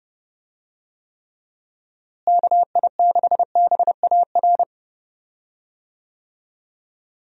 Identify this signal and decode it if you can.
Morse code, heard as a keyed tone, about 31 wpm, 700 Hz. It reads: KI6BAR
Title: USA Call Signs - 6 Characters - Repeated 2x - 30wpm